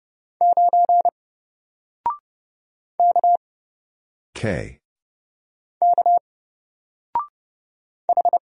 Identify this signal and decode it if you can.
Morse code, heard as a keyed tone, about 31 wpm, 700 Hz. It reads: 9 K K H
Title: Single Letter-Number 30wpm